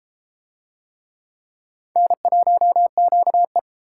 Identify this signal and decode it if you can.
Morse code, heard as a keyed tone, about 29 wpm, 700 Hz. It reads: N1QE